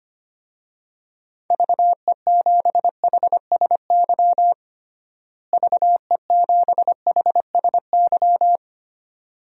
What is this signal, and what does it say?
Morse code, heard as a keyed tone, about 25 wpm, 700 Hz. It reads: VE7HSY VE7HSY
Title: DXCC-100 Call Signs - 6 Characters - Repeated 2x - 25wpm